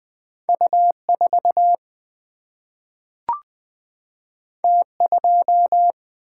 Morse code, heard as a keyed tone, about 20 wpm, 700 Hz. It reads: U 4 T 2
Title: Single-Letter Single-Number 20wpm